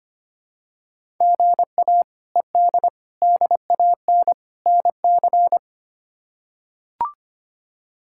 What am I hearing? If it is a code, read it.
Morse code, heard as a keyed tone, about 25 wpm, 700 Hz. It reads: GA ED DAN NC
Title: K1USN SST QSO Practice - 25wpm